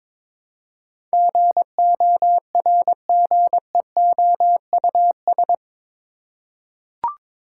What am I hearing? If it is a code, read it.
Morse code, heard as a keyed tone, about 22 wpm, 700 Hz. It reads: GORGEOUS
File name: practice / 22wpm / Eight Letter Words - 22wpm